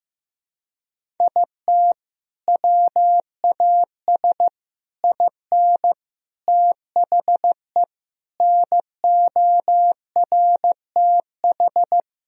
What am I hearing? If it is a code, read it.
Morse code, heard as a keyed tone, about 16 wpm, 700 Hz. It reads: IT WAS IN THE NORTH
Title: Sentences from Top 1000 Words - Review Part 1 - 15wpm